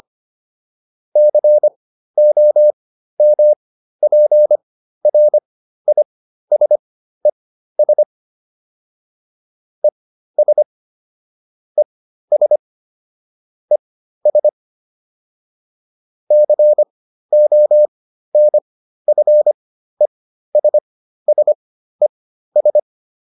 Morse code, heard as a keyed tone, about 25 wpm, spaced as for 13 wpm, 600 Hz. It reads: COMPRISES ES ES ES CONFESSES